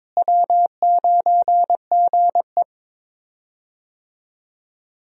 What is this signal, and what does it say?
Morse code, heard as a keyed tone, about 22 wpm, 700 Hz. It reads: W9GE
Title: USA Call Signs - 4 Characters - Repeated 2x - 22wpm